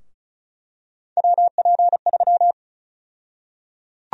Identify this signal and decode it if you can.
Morse code, heard as a keyed tone, about 36 wpm, 700 Hz. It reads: WP3
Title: USA Call Sign Prefixes - Repeated 2x - 35wpm